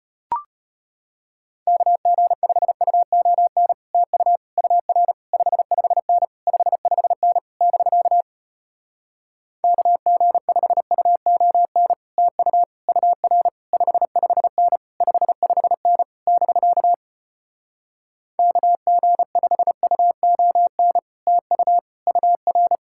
Morse code, heard as a keyed tone, about 33 wpm, 700 Hz. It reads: KG5UON TU UR 55N 55N <BK> KG5UON TU UR 55N 55N <BK> KG5UON TU UR